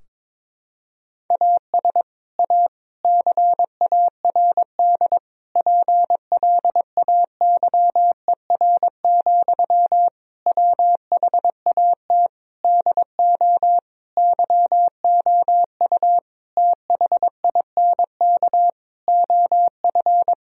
Morse code, heard as a keyed tone, about 22 wpm, 700 Hz. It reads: AS A CARD PLAYER, WHAT DO YOU THINK OF
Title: Sentences from Top 1000 Words - Review Part 4 - 22wpm